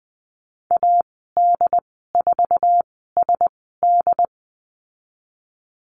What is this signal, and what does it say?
Morse code, heard as a keyed tone, about 20 wpm, spaced as for 15 wpm, 700 Hz. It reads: AD4SD
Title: 076 Lesson 13g - Letter L - Call Signs 20wpm